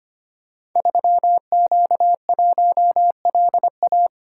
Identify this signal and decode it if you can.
Morse code, heard as a keyed tone, about 25 wpm, 700 Hz. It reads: 3Q1LA
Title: Code Groups - 5 Letter:Numbers - 25wpm